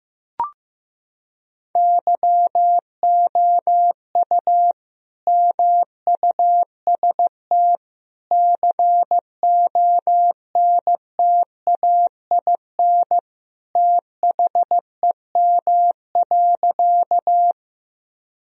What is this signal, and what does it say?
Morse code, heard as a keyed tone, about 15 wpm, 700 Hz. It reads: YOU MUST CONTAIN THEM.
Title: Sentences from Top 1000 Words - Review Part 2 - 15wpm